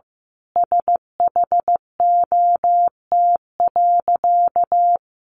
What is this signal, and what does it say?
Morse code, heard as a keyed tone, about 15 wpm, 700 Hz. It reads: SHOT.
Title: A Sound Of Thunder 15wpm